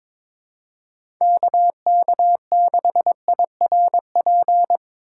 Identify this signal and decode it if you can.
Morse code, heard as a keyed tone, about 22 wpm, 700 Hz. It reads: KK6IRP
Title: USA Call Signs - 6 Characters - 3x Character Spacing - 22wpm